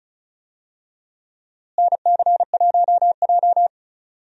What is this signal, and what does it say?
Morse code, heard as a keyed tone, about 36 wpm, 700 Hz. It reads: NC1J
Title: USA Call Signs - 4 Characters - Repeated 3x - 35wpm